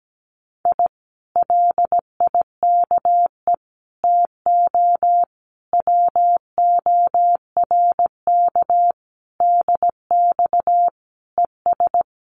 Morse code, heard as a keyed tone, about 17 wpm, 700 Hz. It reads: I LIKE TO WORK DX ES
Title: Common QSO Phrases Encore 17wpm